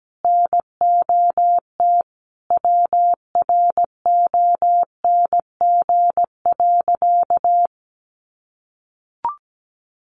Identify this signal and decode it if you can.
Morse code, heard as a keyed tone, about 17 wpm, 700 Hz. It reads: NOT WRONG.
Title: Sentences from Top 900 Words 17wpm